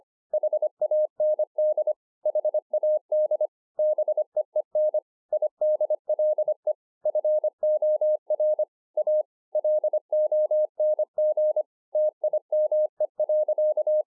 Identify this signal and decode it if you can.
Morse code, heard as a keyed tone, about 25 wpm, 610 Hz. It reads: HAND HAD BEEN IDLE FOR A LONG TIME.